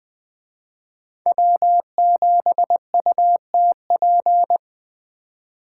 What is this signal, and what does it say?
Morse code, heard as a keyed tone, about 20 wpm, 700 Hz. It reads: W7UTP